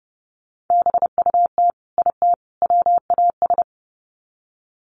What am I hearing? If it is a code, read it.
Morse code, heard as a keyed tone, about 30 wpm, 700 Hz. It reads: BUT IT WAS
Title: Sets of 3 Words 30wpm